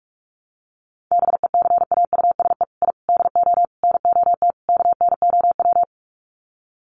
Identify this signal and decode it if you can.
Morse code, heard as a keyed tone, about 44 wpm, 700 Hz. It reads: BECAUSE I DO NOT KNOW
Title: Sets of 5 Words Encore2 45wpm